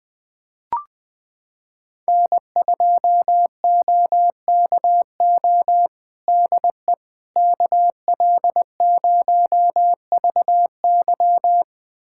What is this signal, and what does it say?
Morse code, heard as a keyed tone, about 20 wpm, 700 Hz. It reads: N2OKO DE KL0VY
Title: USA Callsign DE Callsign 20wpm